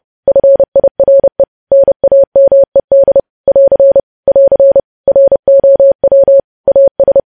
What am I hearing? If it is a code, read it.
Morse code, heard as a keyed tone, about 30 wpm, 550 Hz. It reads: FIRE NAMED <AR> <AR> ROW AS
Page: news headlines in morse code at 30 wpm